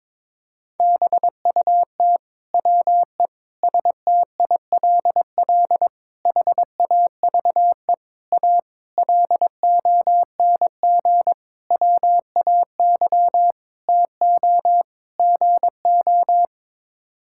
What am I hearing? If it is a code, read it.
Morse code, heard as a keyed tone, about 21 wpm, 700 Hz. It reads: BUT WE STILL HAVE A LONG WAY TO GO
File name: Sets of 9 Words - 22wpm